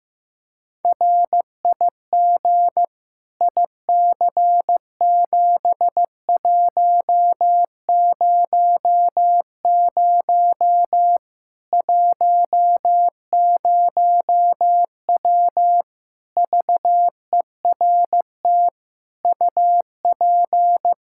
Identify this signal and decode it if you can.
Morse code, heard as a keyed tone, about 15 wpm, 700 Hz. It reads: RIG IC7100 10W VERT UP
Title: Practice QSO - Exchange 2 - Info - 15wpm